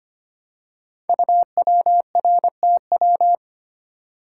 Morse code, heard as a keyed tone, about 25 wpm, 700 Hz. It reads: UWRTW
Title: Code Groups - 5 Letters - 25wpm